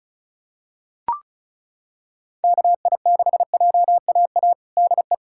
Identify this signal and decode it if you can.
Morse code, heard as a keyed tone, about 36 wpm, 700 Hz. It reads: KI6JAA DE
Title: USA Callsign DE Callsign 35wpm